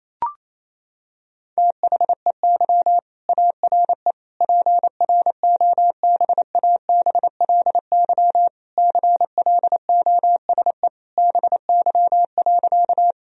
Morse code, heard as a keyed tone, about 28 wpm, 700 Hz. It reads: THEY ARE PROBABLY CLOSE BY.
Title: Sentences from Top 1000 Words - Review Part 4 - 28wpm